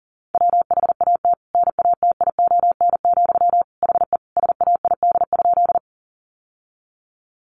Morse code, heard as a keyed tone, about 40 wpm, 700 Hz. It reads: WHAT NATION, HE SAID?
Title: Sentences from Top 900 Words 40wpm